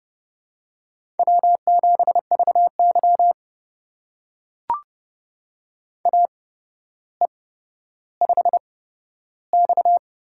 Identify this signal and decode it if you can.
Morse code, heard as a keyed tone, about 30 wpm, spaced as for 13 wpm, 700 Hz. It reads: W7VY AE5X
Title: USA Call Signs - 4 Characters - 8x Character Spacing - 30wpm